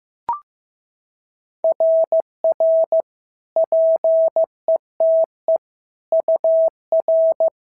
Morse code, heard as a keyed tone, about 15 wpm, 650 Hz. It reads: RR PETE UR
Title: SOTA QSO Practice 15wpm